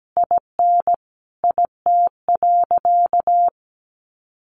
Morse code, heard as a keyed tone, about 17 wpm, 700 Hz. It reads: IN IT.
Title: Sentences from Top 1500 Words 17wpm